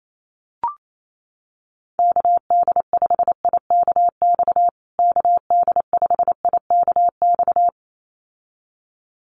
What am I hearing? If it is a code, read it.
Morse code, heard as a keyed tone, about 28 wpm, 700 Hz. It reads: KD5IKX KD5IKX
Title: USA Call Signs Repeated 2x 28wpm